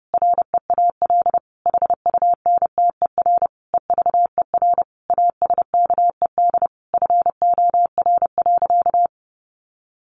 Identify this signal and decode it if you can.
Morse code, heard as a keyed tone, about 30 wpm, 700 Hz. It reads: REAL HUNTER EVER ASKED FOR.
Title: A Sound Of Thunder 30wpm